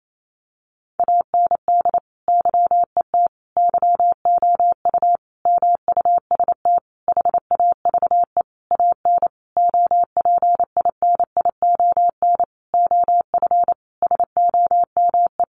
Morse code, heard as a keyed tone, about 28 wpm, 700 Hz. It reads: AND YET YOU MUST HAVE AN OPINION OF SOME
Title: Sentences from Top 1300 Words 28wpm